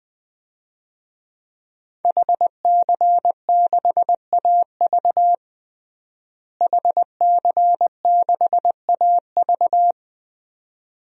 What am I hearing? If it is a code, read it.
Morse code, heard as a keyed tone, about 20 wpm, 700 Hz. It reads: HC6AV HC6AV
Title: DXCC-100 Call Signs - 5 Characters - Repeated 2x - 20wpm